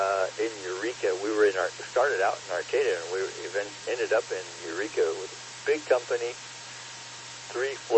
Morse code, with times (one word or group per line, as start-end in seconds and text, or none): none